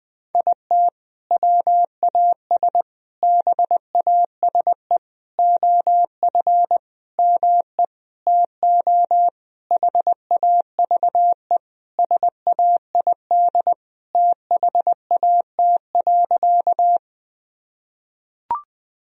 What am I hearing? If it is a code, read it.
Morse code, heard as a keyed tone, about 21 wpm, 700 Hz. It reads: IT WAS BASE OF ME TO HAVE SAID THAT.
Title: Sentences from Top 1000 Words - Review Part 1 - 20wpm